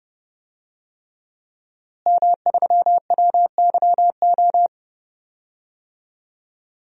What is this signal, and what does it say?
Morse code, heard as a keyed tone, about 30 wpm, 700 Hz. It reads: M3WYO
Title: DXCC-100 Call Signs - 5 Characters - Repeated 3x - 30wpm